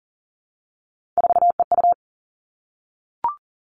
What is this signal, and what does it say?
Morse code, heard as a keyed tone, about 39 wpm, 690 Hz. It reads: 4EU
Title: Three Challenging Letter-Number Combinations 40wpm